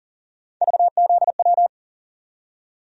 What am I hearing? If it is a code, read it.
Morse code, heard as a keyed tone, about 40 wpm, 700 Hz. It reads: VZW
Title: USA Call Sign Postfixes - 3 Letters 40wpm